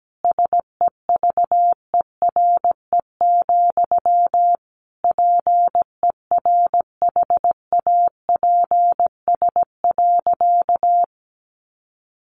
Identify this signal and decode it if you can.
Morse code, heard as a keyed tone, about 17 wpm, 700 Hz. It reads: SEVERE, PERHAPS.
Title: Sentences from Top 1400 Words 17wpm